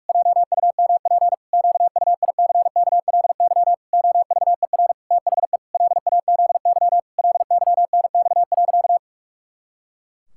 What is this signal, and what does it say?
Morse code, heard as a keyed tone, about 46 wpm, 700 Hz. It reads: JUMP QUICKLY OVER THE LAZY LYNX.
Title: Pangram Challenge v2 - 45wpm